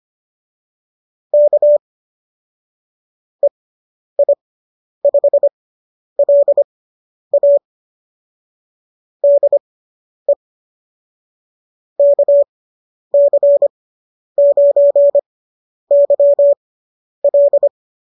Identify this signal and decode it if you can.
Morse code, heard as a keyed tone, about 25 wpm, spaced as for 10 wpm, 570 Hz. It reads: K EI5LA DE KC9YL